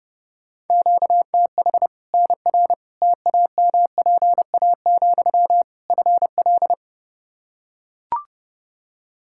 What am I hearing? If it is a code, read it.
Morse code, heard as a keyed tone, about 30 wpm, 700 Hz. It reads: QTH NR TAMPA, FL
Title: Practice QSO - Exchange 1 - Beginning - 30wpm